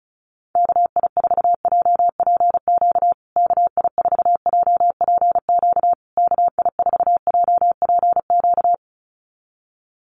Morse code, heard as a keyed tone, about 34 wpm, 700 Hz. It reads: KI4JPQ KI4JPQ KI4JPQ
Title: USA Call Signs Repeated 3x 35wpm